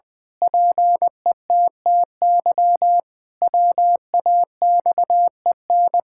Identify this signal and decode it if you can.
Morse code, heard as a keyed tone, about 20 wpm, 700 Hz. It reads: PETTY WAXEN
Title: Pangram Challenge v2 - 20wpm